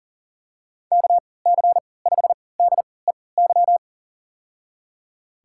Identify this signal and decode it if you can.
Morse code, heard as a keyed tone, about 37 wpm, spaced as for 22 wpm, 700 Hz. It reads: KC5DEY
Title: USA Call Signs - 6 Characters - 3x Character Spacing - 40wpm